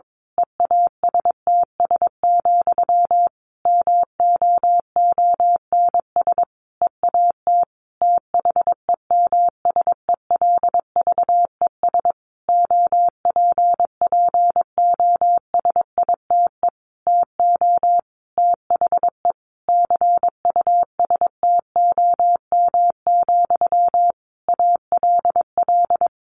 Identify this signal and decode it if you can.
Morse code, heard as a keyed tone, about 22 wpm, 700 Hz. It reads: EASTS, MOONS EAT THEMSELVES OPPOSITE TO THE CUSTOM, ALL
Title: A Sound Of Thunder 22wpm